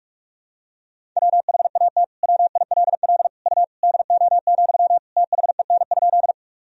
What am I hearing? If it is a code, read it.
Morse code, heard as a keyed tone, about 46 wpm, 700 Hz. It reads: WHAT WILL U DO, THEN?
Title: Sentences from Top 100 Words and Abbreviations 45wpm